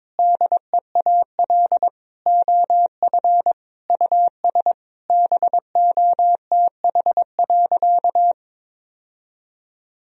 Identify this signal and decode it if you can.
Morse code, heard as a keyed tone, about 22 wpm, 700 Hz. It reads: DEAL OF US BOTH.
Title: Sentences from Top 1000 Words - Review Part 4 - 22wpm